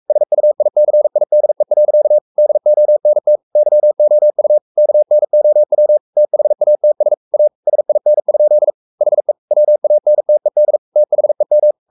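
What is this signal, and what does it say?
Morse code, heard as a keyed tone, about 44 wpm, 600 Hz. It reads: SUICIDE. DONT YOU KNOW THATS A SIN? HE WANTED THEM
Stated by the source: Morse code at 43 wpm sending Ken Follett's novel Never in English, from CW med Gnister.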